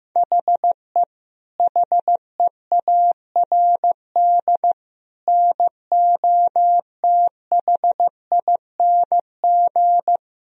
Morse code, heard as a keyed tone, about 15 wpm, 700 Hz. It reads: HE HEARD NOTHING
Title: Sentences from Top 1000 Words - Review Part 3 - 15wpm